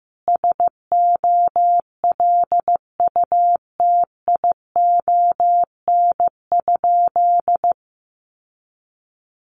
Morse code, heard as a keyed tone, about 15 wpm, 700 Hz. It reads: SOLUTION?
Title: Sentences from Top 1300 Words 15wpm